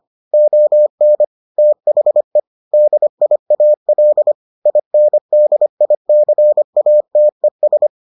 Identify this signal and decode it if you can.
Morse code, heard as a keyed tone, about 25 wpm, 600 Hz. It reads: ON THE DIAL INDICATES